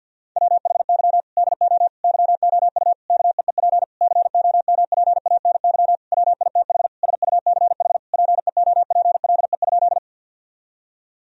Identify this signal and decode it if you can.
Morse code, heard as a keyed tone, about 52 wpm, 700 Hz. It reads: WHY DO YOU KEEP COMPANY WITH SUCH PEOPLE?
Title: Sentences from Top 1000 Words - Review Part 3 - 50wpm